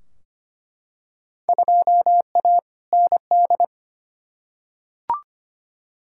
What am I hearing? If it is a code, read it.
Morse code, heard as a keyed tone, about 25 wpm, 700 Hz. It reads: 2A ND
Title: Field Day Exchanges - 25wpm